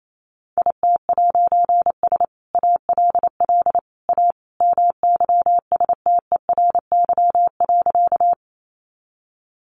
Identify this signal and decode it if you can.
Morse code, heard as a keyed tone, about 28 wpm, 700 Hz. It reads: IT'S ALL A MYSTERY.
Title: A Sound Of Thunder 28wpm